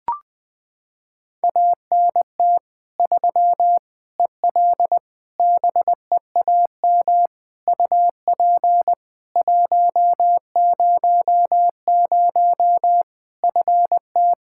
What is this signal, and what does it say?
Morse code, heard as a keyed tone, about 21 wpm, 700 Hz. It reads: ANT 3 EL BEAM UP 100 FT
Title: K1USN SST QSO Practice 20wpm